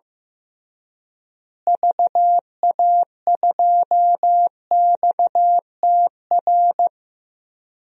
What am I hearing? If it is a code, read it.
Morse code, heard as a keyed tone, about 16 wpm, 700 Hz. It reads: VA2XTR
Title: DXCC-100 Call Signs - 6 Characters - Repeated 3x - 15wpm